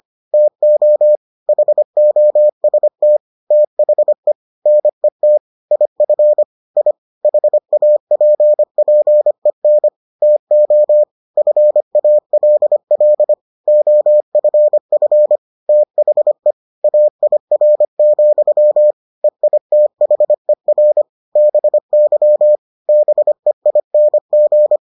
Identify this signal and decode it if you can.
Morse code, heard as a keyed tone, about 25 wpm, 600 Hz. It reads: TO HOST THE NET IF I HAPPEN TO FALL OFF THE AIR, EITHER BY BEING